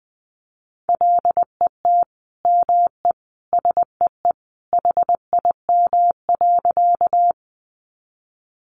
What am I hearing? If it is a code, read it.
Morse code, heard as a keyed tone, about 20 wpm, 700 Hz. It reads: LET ME SEE HIM.
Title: Sentences from Top 300 Words 20wpm